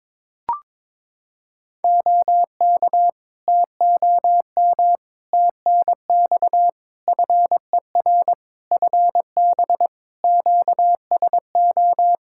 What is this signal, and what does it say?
Morse code, heard as a keyed tone, about 22 wpm, 700 Hz. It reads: OK TOM TNX FER FB QSO
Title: Practice QSO - Exchange 4 - Ending - 22wpm